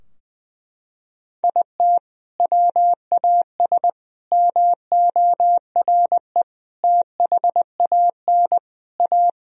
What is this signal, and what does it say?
Morse code, heard as a keyed tone, about 19 wpm, 700 Hz. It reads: IT WAS MORE THAN A
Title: Sentences from Top 1000 Words - Review Part 3 - 20wpm